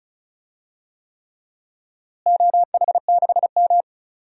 Morse code, heard as a keyed tone, about 36 wpm, 700 Hz. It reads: OH6M
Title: DXCC-100 Call Signs - 4 Characters - Repeated 3x - 35wpm